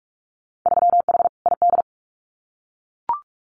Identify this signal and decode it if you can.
Morse code, heard as a keyed tone, about 44 wpm, 700 Hz. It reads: 3H ID
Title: Winter Field Day Exchanges Encore 45wpm